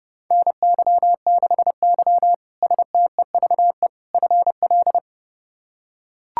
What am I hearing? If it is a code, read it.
Morse code, heard as a keyed tone, about 30 wpm, 700 Hz. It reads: NY6Y STEVE FL
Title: K1USN SST QSO Practice - 30wpm